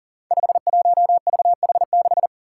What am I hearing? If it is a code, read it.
Morse code, heard as a keyed tone, about 40 wpm, 700 Hz. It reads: 51VH6
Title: Five Challenging Letter-Number Combinations 40wpm